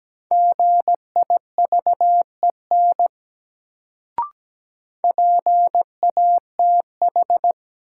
Morse code, heard as a keyed tone, about 17 wpm, 700 Hz. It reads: GIVEN PATH
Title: Sentences from Top 1400 Words - New Word Drill - 17wpm